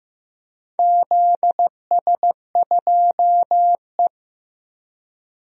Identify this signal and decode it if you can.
Morse code, heard as a keyed tone, about 14 wpm, 700 Hz. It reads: ZS2E
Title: DXCC-100 Call Signs - 4 Characters - Repeated 2x - 15wpm